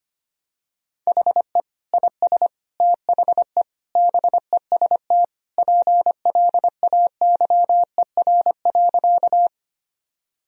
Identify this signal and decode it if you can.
Morse code, heard as a keyed tone, about 25 wpm, 700 Hz. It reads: HE IS THE BEST PLAYER.
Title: Sentences from Top 1000 Words - Review Part 4 - 25wpm